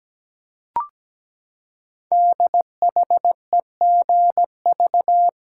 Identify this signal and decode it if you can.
Morse code, heard as a keyed tone, about 17 wpm, 700 Hz. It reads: DHEGV
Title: Code Groups - 5 Letters - 17wpm